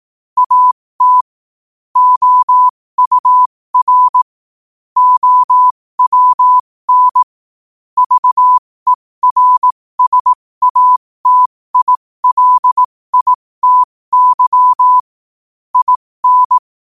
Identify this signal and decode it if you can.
Morse code, heard as a keyed tone, about 18 wpm, spaced as for 15 wpm, 1000 Hz. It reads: AT OUR OWN VERSATILITY IN